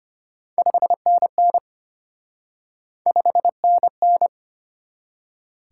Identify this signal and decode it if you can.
Morse code, heard as a keyed tone, about 33 wpm, 700 Hz. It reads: 5NN 5NN